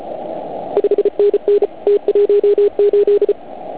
Morse code, heard as a keyed tone, about 33 wpm, 410 Hz. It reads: 5NN T18
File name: Archiv / CUC149 / OK1IF CUC149